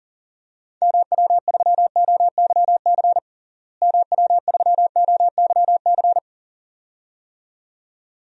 Morse code, heard as a keyed tone, about 40 wpm, 700 Hz. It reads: MW3OYC MW3OYC
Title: DXCC-100 Call Signs - 6 Characters - Repeated 2x - 40wpm